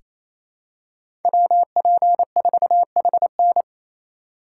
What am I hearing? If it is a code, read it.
Morse code, heard as a keyed tone, about 28 wpm, 700 Hz. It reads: WP4HN
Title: USA Call Signs - 5 Characters - 4x Character Spacing - 28wpm